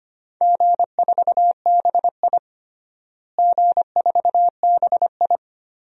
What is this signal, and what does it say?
Morse code, heard as a keyed tone, about 25 wpm, 700 Hz. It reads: G4BI G4BI
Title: DXCC-100 Call Signs - 4 Characters - Repeated 2x - 25wpm